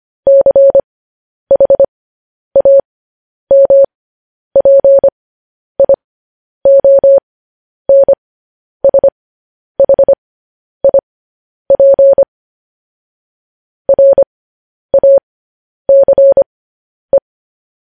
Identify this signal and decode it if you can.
Morse code, heard as a keyed tone, about 25 wpm, spaced as for 10 wpm, 550 Hz. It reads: CHAMPIONSHIP RACE
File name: podcast / News headlines-1x-25x10